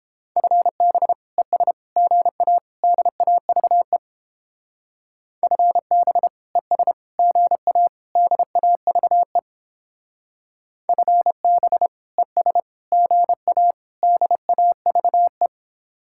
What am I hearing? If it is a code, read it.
Morse code, heard as a keyed tone, about 31 wpm, 700 Hz. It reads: FB ES GA DAVE FB ES GA DAVE FB ES GA DAVE